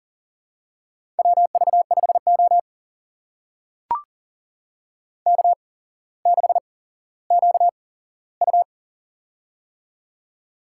Morse code, heard as a keyed tone, about 40 wpm, spaced as for 17 wpm, 700 Hz. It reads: WV5O K6QU
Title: USA Call Signs - 4 Characters - 8x Character Spacing - 40wpm